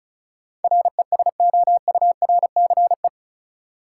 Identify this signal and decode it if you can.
Morse code, heard as a keyed tone, about 34 wpm, 700 Hz. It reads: RESOURCE
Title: Sentences from Top 1300 Words - New Word Drill Alt - 35wpm